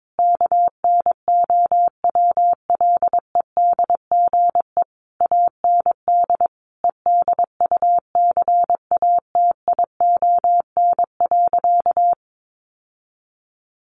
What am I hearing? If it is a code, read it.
Morse code, heard as a keyed tone, about 22 wpm, 700 Hz. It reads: KNOWLEDGE AND EDUCATION.
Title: Sentences from Top 1500 Words 22wpm